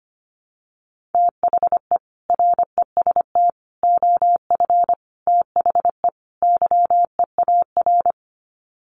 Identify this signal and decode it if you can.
Morse code, heard as a keyed tone, about 25 wpm, 700 Hz. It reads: THE REST OF THE YEAR
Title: Sets of 5 Words Encore 25wpm